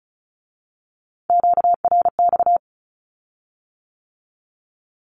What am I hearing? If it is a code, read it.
Morse code, heard as a keyed tone, about 36 wpm, 700 Hz. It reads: QRX
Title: Q-Codes with Spoken Meaning - 35wpm